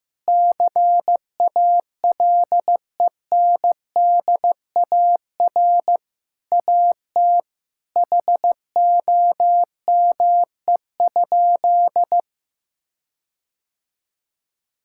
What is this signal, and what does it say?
Morse code, heard as a keyed tone, about 15 wpm, 700 Hz. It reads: CALENDAR AT HOME?
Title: Sentences from Top 1600 Words 15wpm